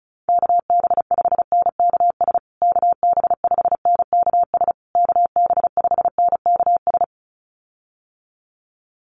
Morse code, heard as a keyed tone, about 34 wpm, 700 Hz. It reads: KB5NKS KB5NKS KB5NKS
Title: USA Call Signs Repeated 3x 35wpm